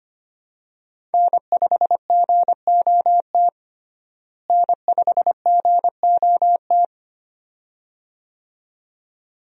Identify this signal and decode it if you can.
Morse code, heard as a keyed tone, about 25 wpm, 700 Hz. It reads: N5GOT N5GOT
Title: USA Call Signs - 5 Characters - Repeated 2x - 25wpm